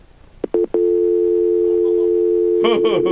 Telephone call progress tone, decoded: dial tone